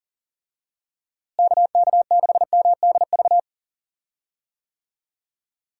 Morse code, heard as a keyed tone, about 40 wpm, 700 Hz. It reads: KK6MDV
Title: USA Call Signs - 6 Characters - Repeated 3x - 40wpm